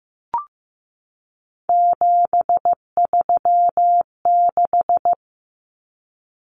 Morse code, heard as a keyed tone, about 15 wpm, 700 Hz. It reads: 736
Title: Common Cut Numbers - 3-Digit 15wpm